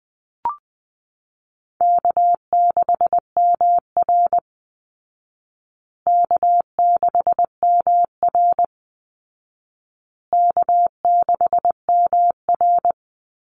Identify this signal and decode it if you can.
Morse code, heard as a keyed tone, about 20 wpm, 700 Hz. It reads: K6MR K6MR K6MR
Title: Easy Call Signs Repeated 3x - 20wpm